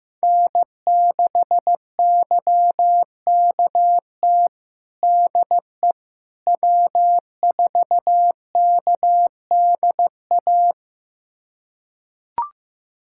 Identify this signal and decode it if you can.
Morse code, heard as a keyed tone, about 15 wpm, 700 Hz. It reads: N6YKT DE W4KDA
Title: USA Callsign DE Callsign - 5 Characters - 15wpm